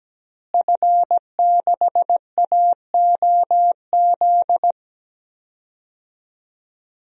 Morse code, heard as a keyed tone, about 17 wpm, 700 Hz. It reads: F6AOZ
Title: DXCC-100 Call Signs - 5 Characters - Repeated 3x - 17wpm